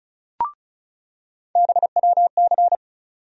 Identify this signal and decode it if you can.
Morse code, heard as a keyed tone, about 36 wpm, 700 Hz. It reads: BWC
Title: USA Call Sign Postfixes - 3 Letters 35wpm